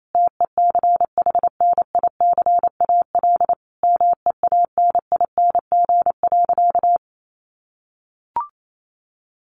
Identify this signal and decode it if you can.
Morse code, heard as a keyed tone, about 28 wpm, 700 Hz. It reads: TECHNICAL MEANING.
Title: Sentences from Top 1100 Words 28wpm